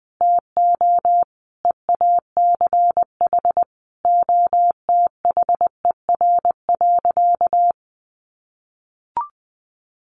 Morse code, heard as a keyed tone, about 20 wpm, 700 Hz. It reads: TO EACH OTHER.
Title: Easy Phrases 20wpm